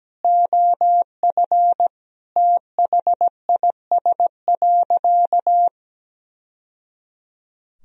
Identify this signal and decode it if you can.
Morse code, heard as a keyed tone, about 17 wpm, 700 Hz. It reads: OF THIS.
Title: Sentences from Top 1000 Words - Review Part 1 - 17wpm